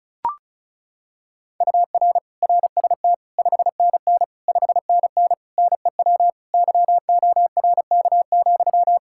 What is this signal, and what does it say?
Morse code, heard as a keyed tone, about 36 wpm, 700 Hz. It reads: UR RST 5NN 5NN NEW YORK,
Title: Practice QSO - Exchange 1 - Beginning - 35wpm